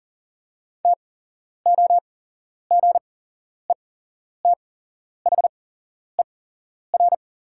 Morse code, heard as a keyed tone, about 40 wpm, spaced as for 11 wpm, 700 Hz. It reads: TOGETHER
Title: 8x Character Spacing - Top 500 Words - 40wpm